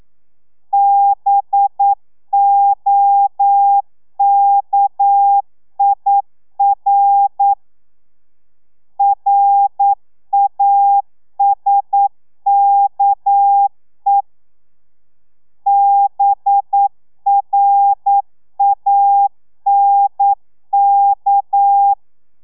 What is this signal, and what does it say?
Morse code, heard as a keyed tone, about 9 wpm, 800 Hz. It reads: BOKIR RASKE BRANK
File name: telegrafia / i2tfj / Lez6 40